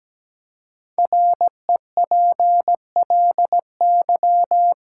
Morse code, heard as a keyed tone, about 17 wpm, 700 Hz. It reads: REPLY
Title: Five Letter Words - 17wpm